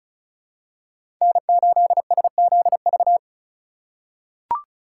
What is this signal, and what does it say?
Morse code, heard as a keyed tone, about 36 wpm, 700 Hz. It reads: N8SZV
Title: USA Call Signs - 5 Characters - 4x Character Spacing - 35wpm